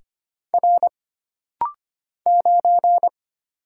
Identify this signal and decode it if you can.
Morse code, heard as a keyed tone, about 25 wpm, 700 Hz. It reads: R 9
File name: Single Character (Letter:Number:Punctuation:Prosign) - Rapid-Fire - 25wpm